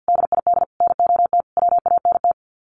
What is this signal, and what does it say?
Morse code, heard as a keyed tone, about 50 wpm, 700 Hz. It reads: DID NOT WANT
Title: Sets of 5 Words Encore 50wpm